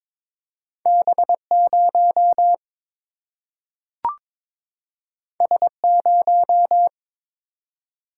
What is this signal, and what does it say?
Morse code, heard as a keyed tone, about 22 wpm, 700 Hz. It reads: B0 S0
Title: Single-Letter Single-Number 22wpm